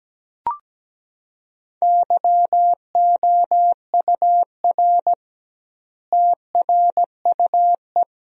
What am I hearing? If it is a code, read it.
Morse code, heard as a keyed tone, about 17 wpm, 700 Hz. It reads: YOUR TRUE
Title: 2x Word-Spacing - Sentences from Top 400 Words - 17wpm